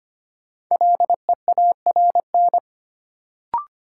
Morse code, heard as a keyed tone, about 25 wpm, 700 Hz. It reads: LEARN
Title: Sentences from Top 300 Words - New Word Drill - 25wpm